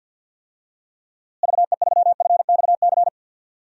Morse code, heard as a keyed tone, about 51 wpm, 700 Hz. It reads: VE3FXC
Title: DXCC-100 Call Signs - 6 Characters - Repeated 3x - 50wpm